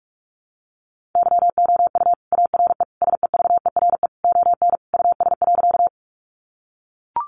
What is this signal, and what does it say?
Morse code, heard as a keyed tone, about 44 wpm, 700 Hz. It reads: YOU ARE SEVERE ON US.
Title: Sentences from Top 1400 Words 45wpm